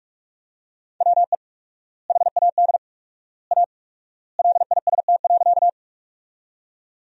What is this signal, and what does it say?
Morse code, heard as a keyed tone, about 42 wpm, 700 Hz. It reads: WE HAD A LIST.
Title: 4x Word-Spacing - Sentences from Top 400 Words - 45wpm